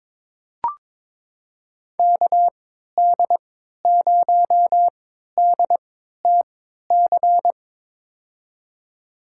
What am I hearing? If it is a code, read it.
Morse code, heard as a keyed tone, about 22 wpm, spaced as for 13 wpm, 700 Hz. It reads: KD0DTC
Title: USA Call Signs - 6 Characters - 3x Character Spacing - 22wpm